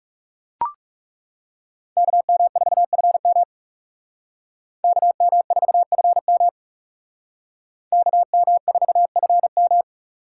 Morse code, heard as a keyed tone, about 36 wpm, 700 Hz. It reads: KM4FM KM4FM KM4FM
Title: Speed Racing - USA Call Signs 30wpm